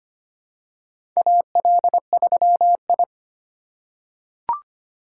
Morse code, heard as a keyed tone, about 25 wpm, 700 Hz. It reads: AL3I
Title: USA Call Signs - 4 Characters - 2x Character Spacing - 25wpm